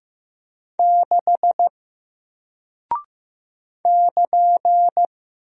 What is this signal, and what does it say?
Morse code, heard as a keyed tone, about 15 wpm, 700 Hz. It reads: 6 <KN>
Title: Single Character (Letter:Number:Punctuation:Prosign) Alt 15wpm